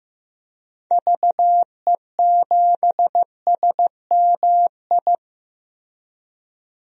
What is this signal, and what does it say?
Morse code, heard as a keyed tone, about 15 wpm, 700 Hz. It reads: VE7SMI
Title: DXCC-100 Call Signs - 6 Characters - Repeated 2x - 15wpm